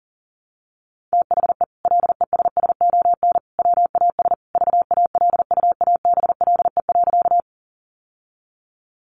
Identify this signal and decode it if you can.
Morse code, heard as a keyed tone, about 40 wpm, 700 Hz. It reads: THE LESSON WAS VALUABLE.
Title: Sentences from Top 1200 Words 40wpm